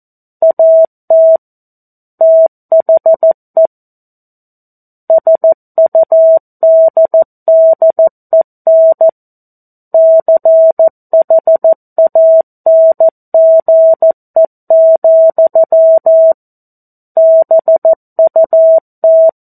Morse code, heard as a keyed tone, about 14 wpm, 650 Hz. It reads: AT THE SUDDEN CHANGE, BUT
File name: 02-Chapter-02 14 WPM